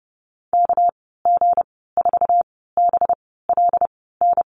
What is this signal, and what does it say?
Morse code, heard as a keyed tone, about 32 wpm, spaced as for 17 wpm, 700 Hz. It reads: KG4BLN
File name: Lesson 37g - Letter x - Call Signs 30wpm